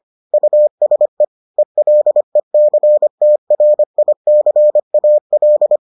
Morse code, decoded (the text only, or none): USE ELECTRICAL